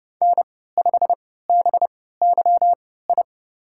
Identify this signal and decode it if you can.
Morse code, heard as a keyed tone, about 30 wpm, spaced as for 17 wpm, 700 Hz. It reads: N5BYI
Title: USA Call Signs - 3x Character Spacing - 30wpm